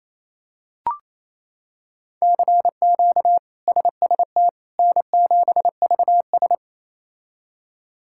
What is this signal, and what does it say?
Morse code, heard as a keyed tone, about 28 wpm, 700 Hz. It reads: CQ SST N7VS
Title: K1USN SST QSO Practice - 28wpm